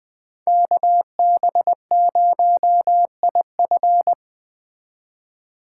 Morse code, heard as a keyed tone, about 20 wpm, 700 Hz. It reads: KB0IF
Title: Speed Racing - USA 2x2 Call Signs 17wpm